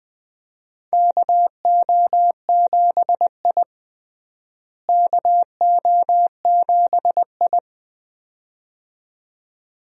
Morse code, heard as a keyed tone, about 20 wpm, 700 Hz. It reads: KO7I KO7I
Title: USA Call Signs - 4 Characters - Repeated 2x - 20wpm